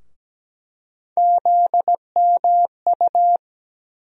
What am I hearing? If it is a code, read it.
Morse code, heard as a keyed tone, about 17 wpm, 700 Hz. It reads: ZMU
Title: USA Call Sign Postfixes - 3 Letters 17wpm